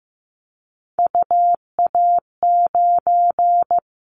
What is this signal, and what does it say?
Morse code, heard as a keyed tone, about 15 wpm, 700 Hz. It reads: UA9
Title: Callsign Prefixes for Top 100 DXCC - 15wpm